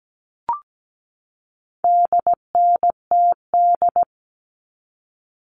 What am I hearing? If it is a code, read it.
Morse code, heard as a keyed tone, about 17 wpm, 700 Hz. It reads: DNTD
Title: All Cut Numbers - 4-Digit 17wpm